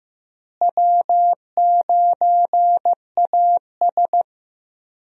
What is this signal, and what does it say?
Morse code, heard as a keyed tone, about 15 wpm, 700 Hz. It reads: W9AS
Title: USA Call Signs - 4 Characters - Repeated 3x - 15wpm